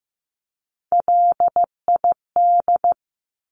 Morse code, heard as a keyed tone, about 15 wpm, 700 Hz. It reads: LID
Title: Abbreviations 15wpm